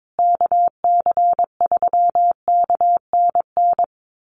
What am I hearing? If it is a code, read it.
Morse code, heard as a keyed tone, about 22 wpm, 700 Hz. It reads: KC3KNN
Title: USA Call Signs Repeated 3x 22wpm